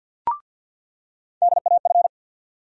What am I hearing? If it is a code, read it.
Morse code, heard as a keyed tone, about 46 wpm, 690 Hz. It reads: DAF
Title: USA Call Sign Postfixes - 3 Letters 50wpm